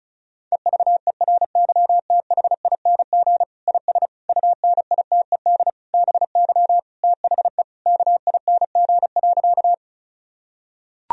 Morse code, heard as a keyed tone, about 36 wpm, 700 Hz. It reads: EVERYTHING IS UNITED BY THE KING.
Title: Sentences from Top 1000 Words - Review Part 4 - 35wpm